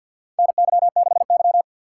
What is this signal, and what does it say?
Morse code, heard as a keyed tone, about 53 wpm, 700 Hz. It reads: NY6Y
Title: K1USN SST QSO Practice - 50wpm